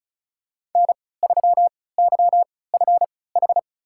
Morse code, heard as a keyed tone, about 36 wpm, spaced as for 20 wpm, 690 Hz. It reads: N3YFH
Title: USA Call Signs - 5 Characters - 3x Character Spacing - 35wpm